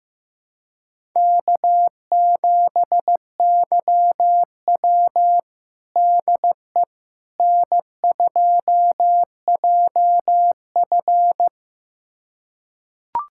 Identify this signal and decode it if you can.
Morse code, heard as a keyed tone, about 16 wpm, 700 Hz. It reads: K7YW DE N2JF
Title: USA Callsign DE Callsign - 4 Characters - 15wpm